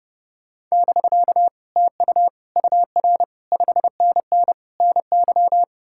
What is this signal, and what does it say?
Morse code, heard as a keyed tone, about 40 wpm, 700 Hz. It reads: <BK> TU UR 5NN NY